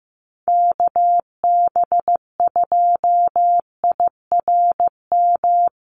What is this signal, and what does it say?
Morse code, heard as a keyed tone, about 15 wpm, 700 Hz. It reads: KB2IRM
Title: USA Call Signs Repeated 2x 15wpm